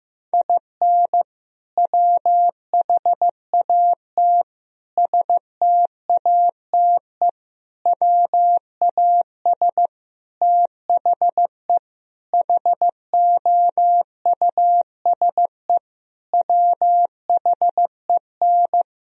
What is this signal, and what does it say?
Morse code, heard as a keyed tone, about 15 wpm, 700 Hz. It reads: IN WHAT STATE WAS THE HOUSE WHEN
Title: Sentences from Top 1000 Words - Review Part 2 - 15wpm